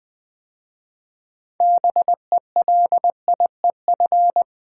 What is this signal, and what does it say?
Morse code, heard as a keyed tone, about 20 wpm, 700 Hz. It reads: BELIEF
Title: 6 Letter Words - First 3 Letters Given - 20wpm